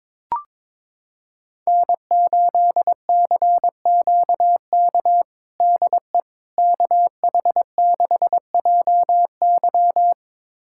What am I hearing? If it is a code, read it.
Morse code, heard as a keyed tone, about 22 wpm, 700 Hz. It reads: N8CQK DE KH6JY
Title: USA Callsign DE Callsign - 5 Characters - 22wpm